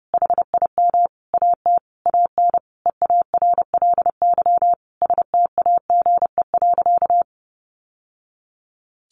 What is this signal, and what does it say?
Morse code, heard as a keyed tone, about 30 wpm, 700 Hz. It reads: HIM AT AN EARLY STAGE.
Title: Sentences from Top 1200 Words 30wpm